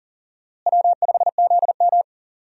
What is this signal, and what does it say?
Morse code, heard as a keyed tone, about 37 wpm, 700 Hz. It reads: W5ZM
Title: USA Call Signs - 4 Characters - Repeated 2x - 40wpm